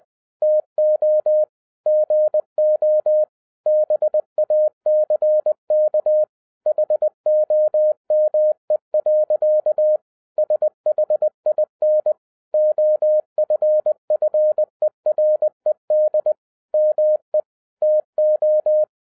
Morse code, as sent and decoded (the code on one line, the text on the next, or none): - --- / --. --- / -... .- -.-. -.- / .... --- -- . .-.-.- / ... .... .. -. / --- ..-. ..-. .-. -.. / -- . / - ---
TO GO BACK HOME. SHIN OFFERED ME TO